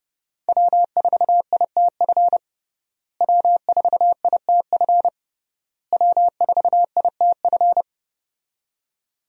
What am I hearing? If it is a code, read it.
Morse code, heard as a keyed tone, about 30 wpm, 700 Hz. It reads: W4ITF W4ITF W4ITF
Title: USA Call Signs - 5 Characters - Repeated 3x - 30wpm